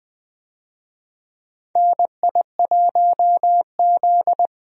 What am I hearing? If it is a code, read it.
Morse code, heard as a keyed tone, about 20 wpm, 700 Hz. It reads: NI1Z
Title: USA Call Signs - 4 Characters - 20wpm